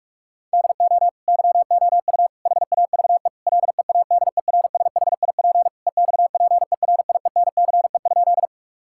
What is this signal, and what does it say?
Morse code, heard as a keyed tone, about 43 wpm, 700 Hz. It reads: DO YOU HAVE LEADERSHIP EXPERIENCE?
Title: Sentences from Top 1600 Words 45wpm